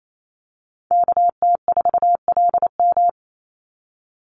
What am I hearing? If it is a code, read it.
Morse code, heard as a keyed tone, about 28 wpm, 700 Hz. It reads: KT4LM
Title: USA Call Signs 28wpm